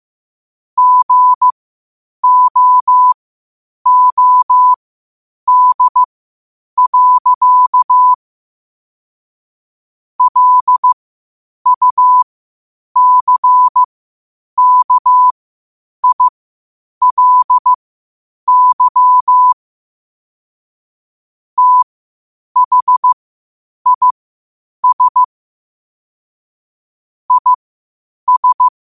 Morse code, heard as a keyed tone, about 15 wpm, spaced as for 8 wpm, 1000 Hz. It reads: GOOD. LUCKILY THIS IS
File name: qotdfiles / 8wpm